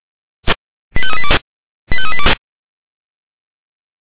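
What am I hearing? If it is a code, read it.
Five-tone selective call (ZVEI), heard as quick runs of five tones, tones of about 70 ms. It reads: 85382 85382
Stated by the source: ZVEI five-tone selective call received by SDR.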